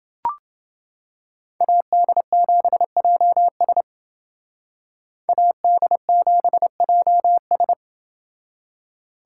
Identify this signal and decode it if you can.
Morse code, heard as a keyed tone, about 29 wpm, 700 Hz. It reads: AD7JS AD7JS